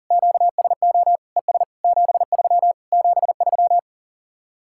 Morse code, heard as a keyed tone, about 49 wpm, 700 Hz. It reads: QSO ES 73 73
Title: Speed-Racing Common QSO Phrases Encore - 40wpm